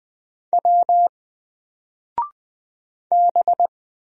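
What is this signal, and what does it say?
Morse code, heard as a keyed tone, about 20 wpm, 700 Hz. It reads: W B
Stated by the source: Single letter and number Morse drill at 20 wpm, from N4LCD.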